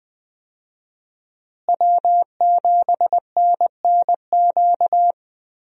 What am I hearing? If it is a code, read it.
Morse code, heard as a keyed tone, about 20 wpm, 700 Hz. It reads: W7NNQ
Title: USA Call Signs - 5 Characters - Repeated 2x - 20wpm